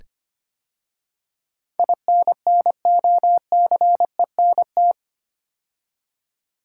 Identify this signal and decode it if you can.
Morse code, heard as a keyed tone, about 25 wpm, 700 Hz. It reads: INNOCENT
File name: 8 Letter Words - First 3 Letters Given - 25wpm